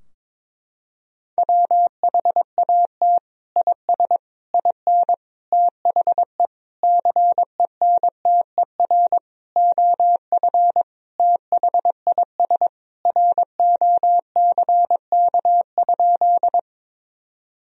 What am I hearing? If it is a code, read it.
Morse code, heard as a keyed tone, about 22 wpm, 700 Hz. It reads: WHAT IS IN THE CENTER OF THIS ROCK?
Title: Sentences from Top 1000 Words - Review Part 2 - 22wpm